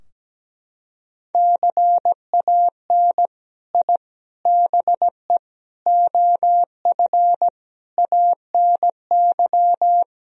Morse code, heard as a keyed tone, about 17 wpm, 700 Hz. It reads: CAN I BE OF ANY
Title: Sentences from Top 1000 Words - Review Part 4 - 17wpm